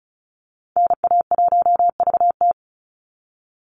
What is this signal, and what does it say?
Morse code, heard as a keyed tone, about 34 wpm, 700 Hz. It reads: NA1VT
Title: Call Signs Repeated 3x - 35wpm